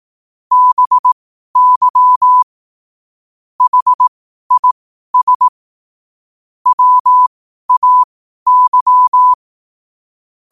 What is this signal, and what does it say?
Morse code, heard as a keyed tone, about 18 wpm, spaced as for 12 wpm, 1000 Hz. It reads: BY HIS WAY
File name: qotdfiles / 12wpm